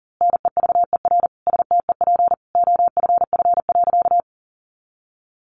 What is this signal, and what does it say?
Morse code, heard as a keyed tone, about 40 wpm, 700 Hz. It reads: NEVER STEP OFF.
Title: A Sound Of Thunder 40wpm